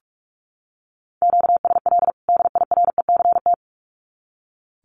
Morse code, heard as a keyed tone, about 45 wpm, 700 Hz. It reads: QSL DIRECT
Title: Common QSO Phrases 45wpm